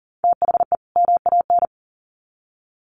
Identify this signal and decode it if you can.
Morse code, heard as a keyed tone, about 40 wpm, 700 Hz. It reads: THE MAN
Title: Sets of 2 Words 40wpm